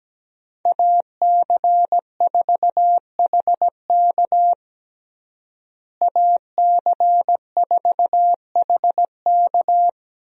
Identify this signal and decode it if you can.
Morse code, heard as a keyed tone, about 17 wpm, 700 Hz. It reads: AC4HK AC4HK
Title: USA Call Signs - 5 Characters - Repeated 2x - 17wpm